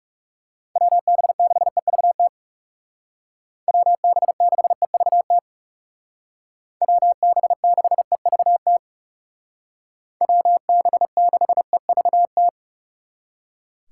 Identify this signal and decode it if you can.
Morse code, heard as a keyed tone, about 35 wpm, 700 Hz. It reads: WB6EVT WB6EVT WB6EVT WB6EVT
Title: Speed Racing - USA 2x3 Call Signs 30wpm